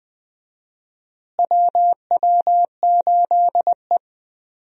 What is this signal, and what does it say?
Morse code, heard as a keyed tone, about 20 wpm, 700 Hz. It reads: WW8E
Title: USA Call Signs - 4 Characters - Repeated 3x - 20wpm